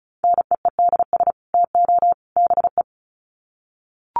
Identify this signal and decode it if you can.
Morse code, heard as a keyed tone, about 34 wpm, 700 Hz. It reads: NEEDS TO BE
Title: Sets of 3 Words 35wpm